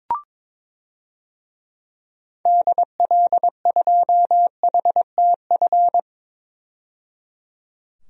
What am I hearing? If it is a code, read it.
Morse code, heard as a keyed tone, about 22 wpm, 700 Hz. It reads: DL2HTF
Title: DXCC-100 Call Signs - 6 Characters - 22wpm